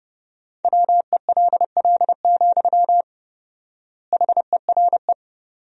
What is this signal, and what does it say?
Morse code, heard as a keyed tone, about 30 wpm, 700 Hz. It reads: WELL, HERE
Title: 4x Word-Spacing - Sentences from Top 200 Words - 30wpm